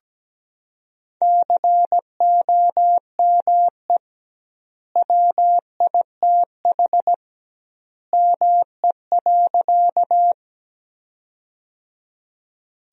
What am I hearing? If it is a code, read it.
Morse code, heard as a keyed tone, about 17 wpm, 700 Hz. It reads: COME WITH ME.
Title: 2x Word-Spacing - Sentences from Top 200 Words - 17wpm